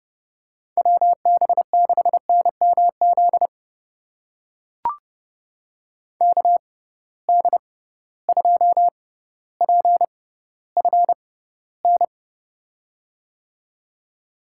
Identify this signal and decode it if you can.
Morse code, heard as a keyed tone, about 30 wpm, spaced as for 15 wpm, 700 Hz. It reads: WB6NMZ KD2PFN
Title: USA Call Signs - 6 Characters - 6x Character Spacing - 30wpm